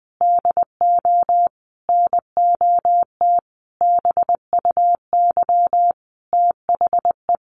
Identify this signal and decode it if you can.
Morse code, heard as a keyed tone, about 20 wpm, 700 Hz. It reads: DO NOT BUY THE
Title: Sentences from Top 1300 Words 20wpm